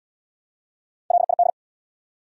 Morse code, heard as a keyed tone, about 69 wpm, 700 Hz. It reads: LID